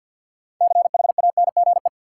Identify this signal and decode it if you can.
Morse code, heard as a keyed tone, about 53 wpm, 700 Hz. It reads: CHANGE
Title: Six Letter Words - 50wpm